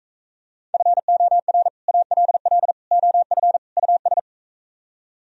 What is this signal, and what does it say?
Morse code, heard as a keyed tone, about 44 wpm, 700 Hz. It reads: FOR ALL OF US